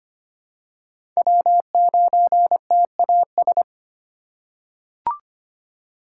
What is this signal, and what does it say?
Morse code, heard as a keyed tone, about 25 wpm, 700 Hz. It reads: W9TAS
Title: USA Call Signs - 5 Characters - 4x Character Spacing - 25wpm